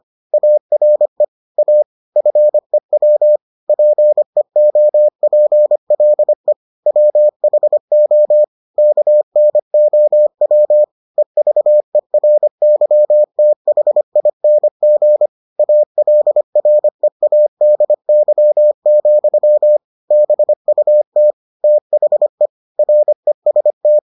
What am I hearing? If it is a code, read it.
Morse code, heard as a keyed tone, about 25 wpm, 600 Hz. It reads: ARE A FEW PEOPLE WHO KNOW EVERYTHING ALREADY, BUT THE REST